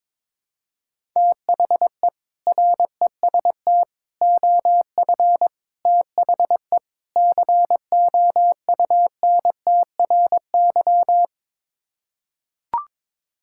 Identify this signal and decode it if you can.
Morse code, heard as a keyed tone, about 22 wpm, 700 Hz. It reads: THE REST OF THE COUNTRY
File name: practice / 22wpm / Build A Phrase - 5-Word Phrases - 22wpm